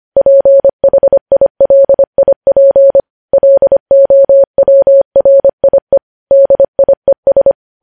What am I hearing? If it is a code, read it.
Morse code, heard as a keyed tone, about 25 wpm, 550 Hz. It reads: PHILIP LOWRIE DIES
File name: podcast / News headlines-1x-25x25